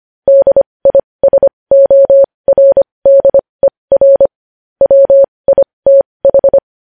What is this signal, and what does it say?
Morse code, heard as a keyed tone, about 25 wpm, spaced as for 20 wpm, 550 Hz. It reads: DISORDER WITH
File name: podcast / News headlines-1x-25x20